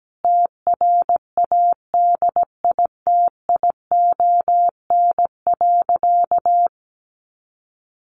Morse code, heard as a keyed tone, about 17 wpm, 700 Hz. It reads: TRADITION.
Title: Sentences from Top 1500 Words 17wpm